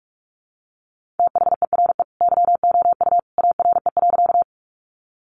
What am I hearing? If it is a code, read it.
Morse code, heard as a keyed tone, about 44 wpm, 700 Hz. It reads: THERE YOU ARE.
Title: Sentences from Top 100 Words 45wpm